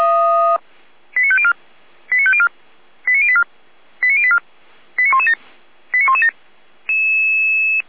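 Five-tone selective call (ZVEI), heard as quick runs of five tones, tones of about 80 ms. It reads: 89504 89504 89084 89084 89117 89117
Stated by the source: ZVEI five-tone selective call 89115 received on SDR.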